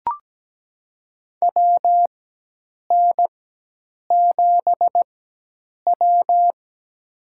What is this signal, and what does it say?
Morse code, heard as a keyed tone, about 17 wpm, spaced as for 8 wpm, 700 Hz. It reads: WN7W
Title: USA Call Signs - 4x Character Spacing - 17wpm